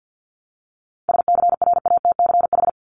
Morse code, heard as a keyed tone, about 50 wpm, 700 Hz. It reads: SCRATCH
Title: Top 1501-1750 Words 50wpm